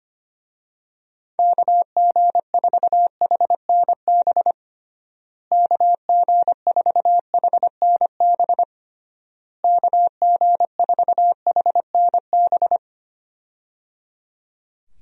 Morse code, heard as a keyed tone, about 25 wpm, 700 Hz. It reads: KG4HNB KG4HNB KG4HNB
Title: USA Call Signs - 6 Characters - Repeated 3x - 25wpm